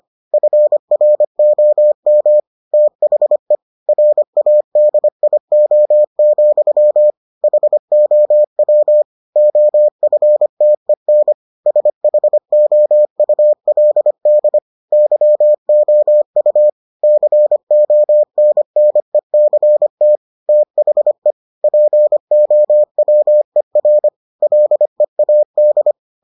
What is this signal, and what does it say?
Morse code, heard as a keyed tone, about 25 wpm, 600 Hz. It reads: FROM THE RADIO, HOW OFTEN SHOULD YOU CONNECT THE POWER LEAD